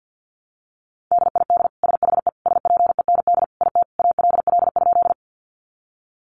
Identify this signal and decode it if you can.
Morse code, heard as a keyed tone, about 50 wpm, 700 Hz. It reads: DID SHE SPEND IT ALL?
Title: Sentences from Top 800 Words 50wpm